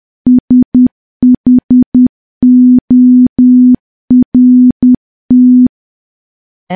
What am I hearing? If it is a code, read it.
Morse code, heard as a keyed tone, about 10 wpm, 250 Hz. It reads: SHORT
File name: morse / morse-training-10WPM